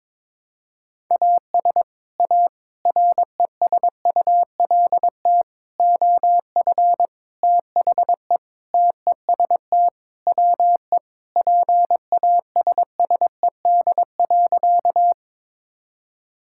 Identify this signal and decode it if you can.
Morse code, heard as a keyed tone, about 22 wpm, 700 Hz. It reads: AS A RESULT OF THE TEST WE PASSED.
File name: practice / 22wpm / Sentences with 4-Word Phrases - 22wpm